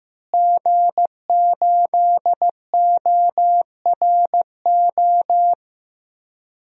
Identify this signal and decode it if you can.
Morse code, heard as a keyed tone, about 15 wpm, 700 Hz. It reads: G8ORO
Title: DXCC-100 Call Signs - 5 Characters - Repeated 3x - 15wpm